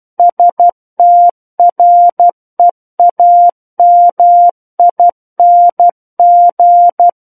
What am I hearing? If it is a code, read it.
Morse code, heard as a keyed tone, about 12 wpm, 700 Hz. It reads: STREAMING
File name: Book One / 03-Chapter 3 - 12 WPM